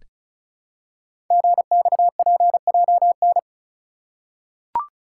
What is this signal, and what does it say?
Morse code, heard as a keyed tone, about 36 wpm, 690 Hz. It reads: GXPJN
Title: Code Groups - 5 Letter:Numbers - 35wpm